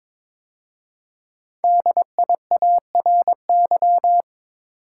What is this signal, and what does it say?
Morse code, heard as a keyed tone, about 22 wpm, 700 Hz. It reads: DIARY
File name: 5 Letter Words - First 3 Letters Given - 22wpm